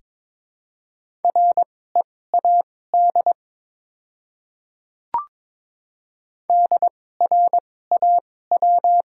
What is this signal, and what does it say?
Morse code, heard as a keyed tone, about 22 wpm, spaced as for 16 wpm, 700 Hz. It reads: READ DRAW
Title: 2x Character Spacing - Top 300 Words - 22wpm